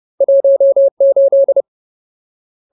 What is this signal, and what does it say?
Morse code, heard as a keyed tone, about 30 wpm, 550 Hz. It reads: 18
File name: Ultimate Word List / 30WPM Ultimate